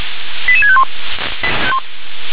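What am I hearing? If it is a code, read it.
Five-tone selective call (ZVEI), heard as quick runs of five tones, tones of about 70 ms. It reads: 88651
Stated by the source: Five-tone ZVEI selective call, received via SDR.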